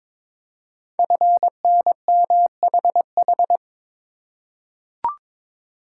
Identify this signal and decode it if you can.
Morse code, heard as a keyed tone, about 22 wpm, 700 Hz. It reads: FNMHH
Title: Code Groups - 5 Letters - 22wpm